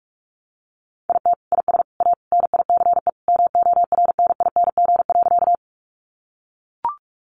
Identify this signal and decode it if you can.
Morse code, heard as a keyed tone, about 44 wpm, 700 Hz. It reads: IT IS A NICE MORNING.
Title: Sentences from Top 1100 Words 45wpm